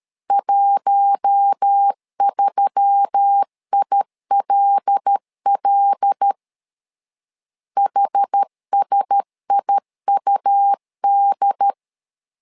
Morse code, heard as a keyed tone, about 13 wpm, 800 Hz. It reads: 13ILL HSIUD
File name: LEZIONI / 5 LBVC / 5 L 12wpm 1913